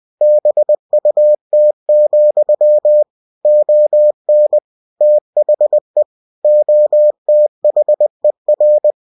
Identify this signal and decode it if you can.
Morse code, heard as a keyed tone, about 20 wpm, 600 Hz. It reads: BUT, ON THE OTHER
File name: cw-books / winnie-the-pooh / winnie-20wpm-page-0034